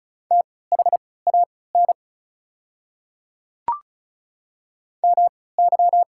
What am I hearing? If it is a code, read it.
Morse code, heard as a keyed tone, about 36 wpm, spaced as for 20 wpm, 700 Hz. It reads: THAN MY
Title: 3x Character Spacing - Top 100 Words - 35wpm